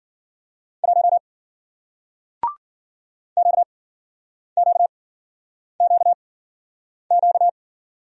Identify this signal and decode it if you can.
Morse code, heard as a keyed tone, about 54 wpm, spaced as for 9 wpm, 700 Hz. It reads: 1 QQQQ